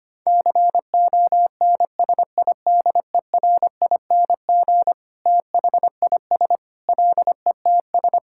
Morse code, heard as a keyed tone, about 25 wpm, 700 Hz. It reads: CONSIDERING THIS LETS
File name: practice / 25wpm / Sentences with 9-Word Phrases - 25wpm